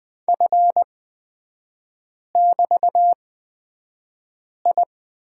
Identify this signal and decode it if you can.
Morse code, heard as a keyed tone, about 20 wpm, spaced as for 5 wpm, 690 Hz. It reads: F<BT>I
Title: 229 Lesson 37e - Letter x - Three Characters 20wpm